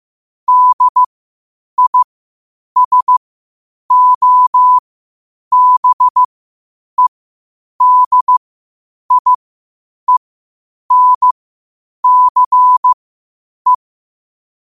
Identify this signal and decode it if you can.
Morse code, heard as a keyed tone, about 15 wpm, spaced as for 8 wpm, 1000 Hz. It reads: DISOBEDIENCE
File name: qotdfiles / 8wpm